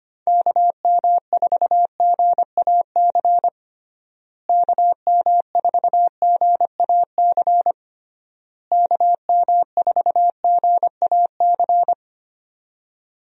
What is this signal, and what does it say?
Morse code, heard as a keyed tone, about 25 wpm, 700 Hz. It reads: KM4GAC KM4GAC KM4GAC
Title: USA Call Signs - 6 Characters - Repeated 3x - 25wpm